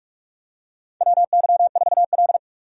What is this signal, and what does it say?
Morse code, heard as a keyed tone, about 41 wpm, 700 Hz. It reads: WY4L
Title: USA Call Signs - 4 Characters - 45wpm